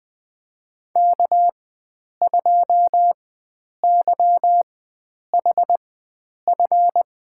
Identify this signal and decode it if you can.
Morse code, heard as a keyed tone, about 20 wpm, spaced as for 9 wpm, 700 Hz. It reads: K2YHF
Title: USA Call Signs - 5 Characters - 4x Character Spacing - 20wpm